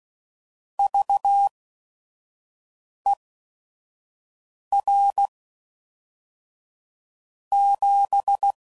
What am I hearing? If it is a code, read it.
Morse code, heard as a keyed tone, about 16 wpm, spaced as for 5 wpm, 800 Hz. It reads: VER 7
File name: CW Code Groups @5WPM / QSOs / kc7sac